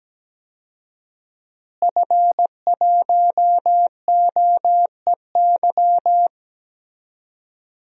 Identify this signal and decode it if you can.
Morse code, heard as a keyed tone, about 17 wpm, 700 Hz. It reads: F1OEY
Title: DXCC-100 Call Signs - 5 Characters - Repeated 3x - 17wpm